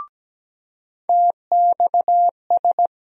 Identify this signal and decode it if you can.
Morse code, heard as a keyed tone, about 17 wpm, 700 Hz. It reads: TXS
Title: USA Call Sign Postfixes - 3 Letters 17wpm